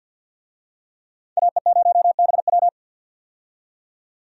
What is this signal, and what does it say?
Morse code, heard as a keyed tone, about 46 wpm, 700 Hz. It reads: AE0BW
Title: USA Call Signs - 5 Characters - 50wpm